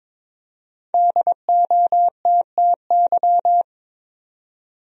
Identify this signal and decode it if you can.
Morse code, heard as a keyed tone, about 22 wpm, 700 Hz. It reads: DOTTY
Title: USA Common Names - 22wpm